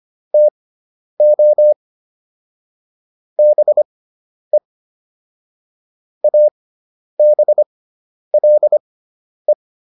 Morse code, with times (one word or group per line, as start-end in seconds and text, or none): 0.3-1.7 TO
3.4-4.6 BE
6.2-9.5 ABLE